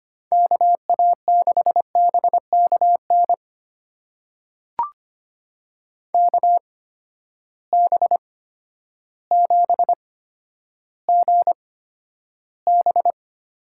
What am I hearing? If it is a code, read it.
Morse code, heard as a keyed tone, about 25 wpm, spaced as for 11 wpm, 700 Hz. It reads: KA6BKN KB7GB
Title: USA Call Signs - 6 Characters - 8x Character Spacing - 25wpm